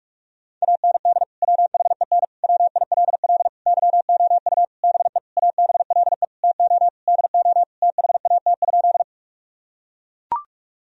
Morse code, heard as a keyed tone, about 42 wpm, 700 Hz. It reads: AND WHEN WILL YOU BE ABLE TO DO THAT?
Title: Sentences from Top 1000 Words - Review Part 2 - 45wpm